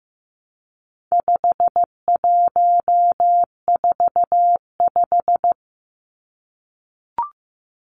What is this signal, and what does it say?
Morse code, heard as a keyed tone, about 15 wpm, 700 Hz. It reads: 5145
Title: Numbers - 4-Digit 15wpm